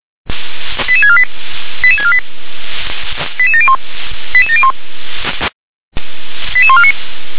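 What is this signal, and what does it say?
Five-tone selective call (ZVEI), heard as quick runs of five tones, tones of about 70 ms. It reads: 88648 88648 88701 88701 88149
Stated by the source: ZVEI five-tone selective call received by SDR.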